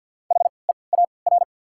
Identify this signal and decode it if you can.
Morse code, heard as a keyed tone, about 54 wpm, spaced as for 29 wpm, 690 Hz. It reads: HEAR
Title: 3x Character Spacing - Top 500 Words - 50wpm